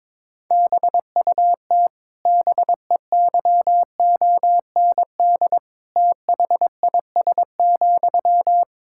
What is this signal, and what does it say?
Morse code, heard as a keyed tone, about 21 wpm, 700 Hz. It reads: BUT BEYOND THIS,
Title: Sentences from Top 1000 Words - Review Part 6 - 22wpm